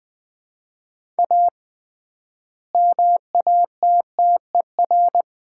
Morse code, heard as a keyed tone, about 20 wpm, 700 Hz. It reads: A MATTER
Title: 3x Word-Spacing - Sets of 4 Words - 20wpm